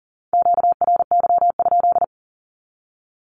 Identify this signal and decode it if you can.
Morse code, heard as a keyed tone, about 40 wpm, 700 Hz. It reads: QRY?
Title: Q-Codes 40wpm